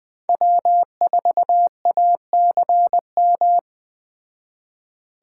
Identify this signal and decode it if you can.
Morse code, heard as a keyed tone, about 20 wpm, 700 Hz. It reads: W4ACM
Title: Speed Racing - USA 1x3 Call Signs 17wpm